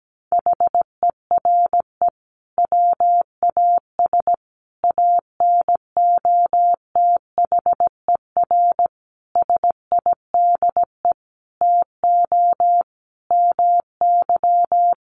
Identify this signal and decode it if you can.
Morse code, heard as a keyed tone, about 17 wpm, 700 Hz. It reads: HERE WAS ANOTHER SIDE TO MY
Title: Sentences from Top 1500 Words 17wpm